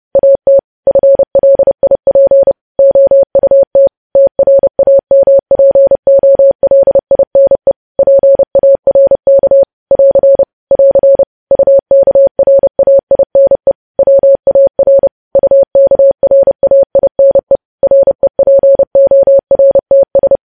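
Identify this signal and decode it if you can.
Morse code, heard as a keyed tone, about 30 wpm, 550 Hz. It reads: AT FLIP OUT TRAMPOLINE PARK <AR> <AR> UKRAINE WAR UKRAINE REPORTS
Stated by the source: News headlines sent in Morse at 30 wpm, preceded by VVV.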